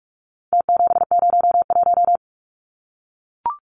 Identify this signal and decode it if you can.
Morse code, heard as a keyed tone, about 44 wpm, 700 Hz. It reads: T701
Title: Numbers - Contest Style - 4-Digit 45wpm